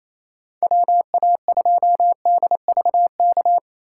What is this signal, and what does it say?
Morse code, heard as a keyed tone, about 27 wpm, 700 Hz. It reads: WA2DVK
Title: USA Call Signs - 6 Characters - 3x Character Spacing - 28wpm